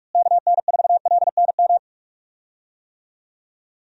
Morse code, heard as a keyed tone, about 44 wpm, 700 Hz. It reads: KN4LNM
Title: USA Call Signs - 6 Characters - Repeated 2x - 45wpm